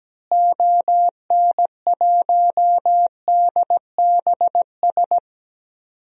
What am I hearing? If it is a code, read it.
Morse code, heard as a keyed tone, about 17 wpm, 700 Hz. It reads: ON1DBS
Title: DXCC-100 Call Signs - 6 Characters - Repeated 2x - 17wpm